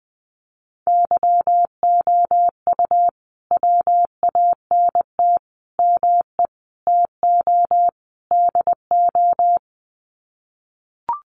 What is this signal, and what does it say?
Morse code, heard as a keyed tone, about 20 wpm, 700 Hz. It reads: YOU WANT ME TO DO
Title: Sets of 5 Words Encore2 20wpm